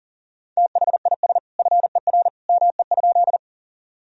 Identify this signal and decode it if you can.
Morse code, heard as a keyed tone, about 40 wpm, 700 Hz. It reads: THIS FER ME?
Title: Sentences from Top 200 Words and Abbreviations 40wpm